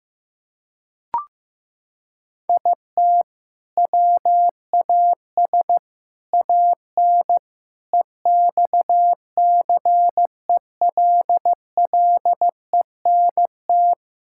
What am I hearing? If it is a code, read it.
Morse code, heard as a keyed tone, about 15 wpm, 700 Hz. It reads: IT WAS AN EXCELLENT
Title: Sentences from Top 1000 Words - Review Part 5 - 15wpm